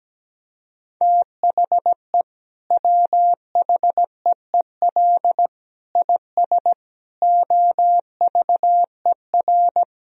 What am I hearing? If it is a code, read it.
Morse code, heard as a keyed tone, about 17 wpm, 700 Hz. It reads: THE WHEEL IS OVER
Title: Sentences from Top 1000 Words - Review Part 3 - 17wpm